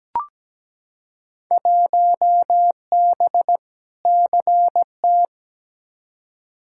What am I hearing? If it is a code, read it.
Morse code, heard as a keyed tone, about 17 wpm, 700 Hz. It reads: 1B CT
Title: Field Day Exchanges Encore - 17wpm